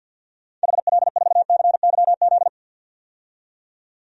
Morse code, heard as a keyed tone, about 49 wpm, 700 Hz. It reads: HL4CYZ
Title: DXCC-100 Call Signs - 6 Characters - 50wpm